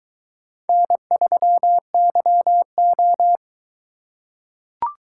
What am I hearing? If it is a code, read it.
Morse code, heard as a keyed tone, about 22 wpm, 700 Hz. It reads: N3YO